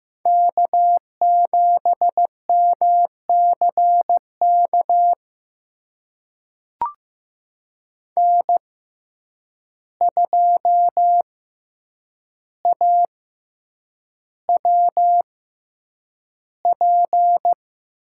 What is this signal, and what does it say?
Morse code, heard as a keyed tone, about 15 wpm, spaced as for 8 wpm, 700 Hz. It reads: K7MCK N2AWP
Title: USA Call Signs - 5 Characters - 6x Character Spacing - 15wpm